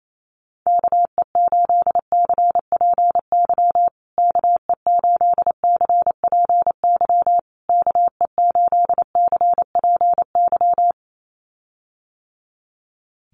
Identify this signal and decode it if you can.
Morse code, heard as a keyed tone, about 28 wpm, 690 Hz. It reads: KE8CPY KE8CPY KE8CPY
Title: USA Call Signs Repeated 3x 28wpm